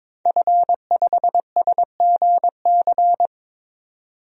Morse code, heard as a keyed tone, about 22 wpm, 700 Hz. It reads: F5SGC
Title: DXCC-100 Call Signs - 5 Characters - 22wpm